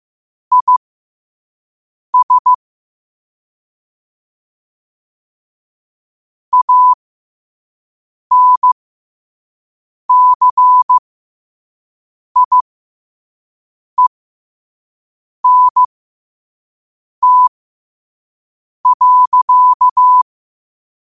Morse code, heard as a keyed tone, about 15 wpm, spaced as for 5 wpm, 1000 Hz. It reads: IS ANCIENT.
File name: 5wpm